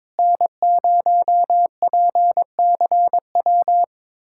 Morse code, heard as a keyed tone, about 22 wpm, 700 Hz. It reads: N0PCW
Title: USA Callsign DE Callsign - 5 Characters - 22wpm